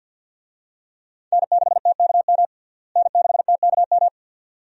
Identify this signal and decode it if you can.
Morse code, heard as a keyed tone, about 51 wpm, 700 Hz. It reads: N6TKM N6TKM
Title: USA Call Signs - 5 Characters - Repeated 2x - 50wpm